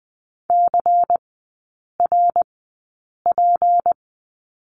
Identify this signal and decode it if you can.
Morse code, heard as a keyed tone, about 20 wpm, spaced as for 8 wpm, 700 Hz. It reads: CRP
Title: 166 Lesson 27e - Number 7 - Three Characters 20wpm